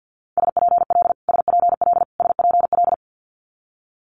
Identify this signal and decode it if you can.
Morse code, heard as a keyed tone, about 50 wpm, 700 Hz. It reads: SPL SPL SPL
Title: Multi-letter Phonemes Repeated 3x 50wpm